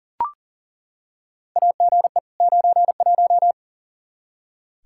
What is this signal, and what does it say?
Morse code, heard as a keyed tone, about 40 wpm, 700 Hz. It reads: AGE 91
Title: Practice QSO - Exchange 3 - More Info - 40wpm